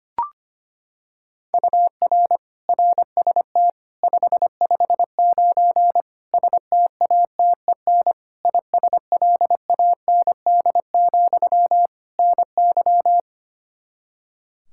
Morse code, heard as a keyed tone, about 25 wpm, 700 Hz. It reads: UR RST 559 STATEN ISLAND, NY
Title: Practice QSO - Exchange 1 - Beginning - 25wpm